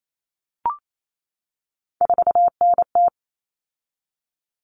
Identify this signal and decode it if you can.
Morse code, heard as a keyed tone, about 28 wpm, 700 Hz. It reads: 4NT
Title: Common Cut Numbers - 3-Digit 28wpm